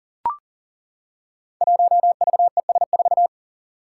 Morse code, heard as a keyed tone, about 40 wpm, 700 Hz. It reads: 1VES4
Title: Five Challenging Letter-Number Combinations 40wpm